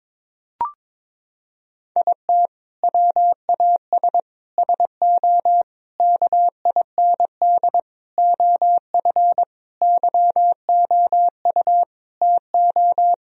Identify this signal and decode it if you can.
Morse code, heard as a keyed tone, about 22 wpm, 700 Hz. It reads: IT WAS SO KIND OF YOU TO
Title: Sentences from Top 1000 Words - Review Part 3 - 22wpm